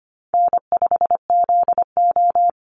Morse code, heard as a keyed tone, about 25 wpm, 700 Hz. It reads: N5ZO
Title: Call Signs - 25wpm